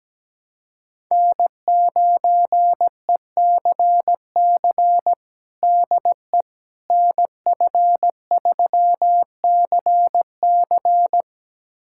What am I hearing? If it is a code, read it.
Morse code, heard as a keyed tone, about 17 wpm, 700 Hz. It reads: N9ECC DE NF3CC
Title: USA Callsign DE Callsign 17wpm